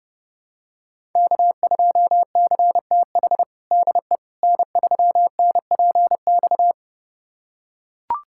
Morse code, heard as a keyed tone, about 30 wpm, 700 Hz. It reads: K2CTH DE N3NPX
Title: USA Callsign DE Callsign - 5 Characters - 30wpm